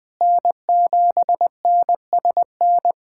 Morse code, heard as a keyed tone, about 20 wpm, 700 Hz. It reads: N7NSN
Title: USA Callsign DE Callsign - 5 Characters - 20wpm